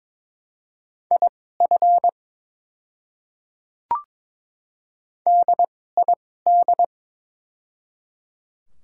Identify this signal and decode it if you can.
Morse code, heard as a keyed tone, about 22 wpm, spaced as for 16 wpm, 700 Hz. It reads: IF DID
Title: 2x Character Spacing - Top 200 Words - 22wpm